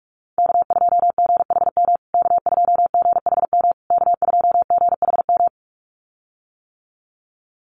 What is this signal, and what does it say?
Morse code, heard as a keyed tone, about 44 wpm, 700 Hz. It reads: K2GHM K2GHM K2GHM
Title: USA Call Signs Repeated 3x 45wpm